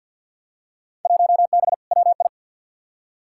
Morse code, heard as a keyed tone, about 48 wpm, 700 Hz. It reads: 1B WI
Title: Field Day Exchanges Encore - 50wpm